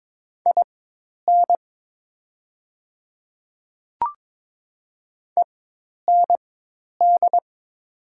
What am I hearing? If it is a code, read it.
Morse code, heard as a keyed tone, about 22 wpm, spaced as for 10 wpm, 700 Hz. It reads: IN END
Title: 4x Character Spacing - Top 500 Words - 22wpm